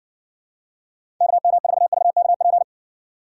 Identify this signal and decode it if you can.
Morse code, heard as a keyed tone, about 62 wpm, 700 Hz. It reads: KM4VKP